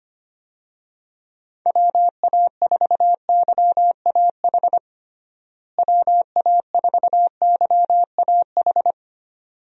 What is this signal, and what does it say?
Morse code, heard as a keyed tone, about 25 wpm, 700 Hz. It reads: WA4YAH WA4YAH
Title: USA Call Signs - 6 Characters - Repeated 2x - 25wpm